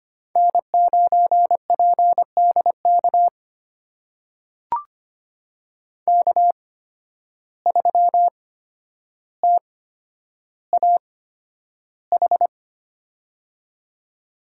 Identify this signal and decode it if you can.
Morse code, heard as a keyed tone, about 25 wpm, spaced as for 11 wpm, 700 Hz. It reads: N9PDK K3TAH
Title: USA Call Signs - 5 Characters - 8x Character Spacing - 25wpm